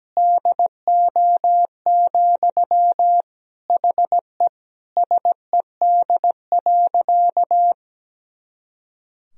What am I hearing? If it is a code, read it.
Morse code, heard as a keyed tone, about 17 wpm, 700 Hz. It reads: DO, HE SED.
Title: Sentences from Top 100 Words and Abbreviations 17wpm